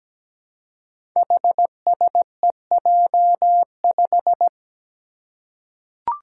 Morse code, heard as a keyed tone, about 17 wpm, 700 Hz. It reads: HSEJ5
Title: Five Challenging Letter-Number Combinations 17wpm